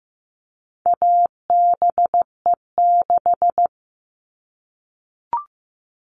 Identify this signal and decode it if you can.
Morse code, heard as a keyed tone, about 15 wpm, 700 Hz. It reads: ABE6
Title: All Cut Numbers - 4-Digit 15wpm